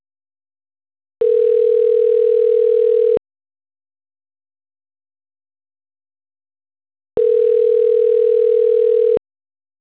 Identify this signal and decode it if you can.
Ringback tone (North America), 440 + 480 Hz.